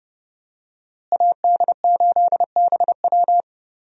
Morse code, heard as a keyed tone, about 30 wpm, 700 Hz. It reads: AD8BW
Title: USA Call Signs - 5 Characters - Repeated 2x - 30wpm